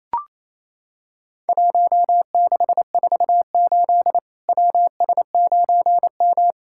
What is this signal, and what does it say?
Morse code, heard as a keyed tone, about 28 wpm, 700 Hz. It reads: 1648 WS9M
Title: Worked All Europe DX Contest - QTC Exchange - 28wpm